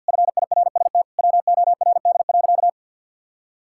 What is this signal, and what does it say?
Morse code, heard as a keyed tone, about 52 wpm, 700 Hz. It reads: FIRST WORD.
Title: Pangram Challenge v2 - 50wpm